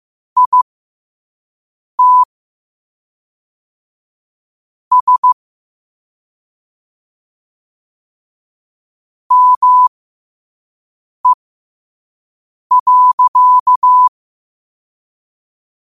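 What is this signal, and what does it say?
Morse code, heard as a keyed tone, about 15 wpm, spaced as for 5 wpm, 1000 Hz. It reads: IT S ME.
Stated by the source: Morse code quote of the day at 5 wpm.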